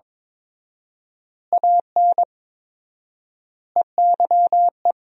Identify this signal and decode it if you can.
Morse code, heard as a keyed tone, about 22 wpm, 700 Hz. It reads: AN EYE
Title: 4x Word-Spacing - Sentences from Top 300 Words - 22wpm